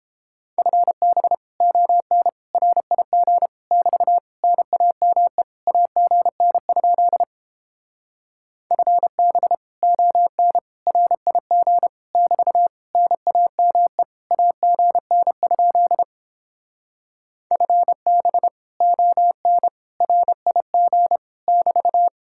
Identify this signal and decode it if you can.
Morse code, heard as a keyed tone, about 29 wpm, 700 Hz. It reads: FB ON RIG <BT> NAME AGN? FB ON RIG <BT> NAME AGN? FB ON RIG <BT>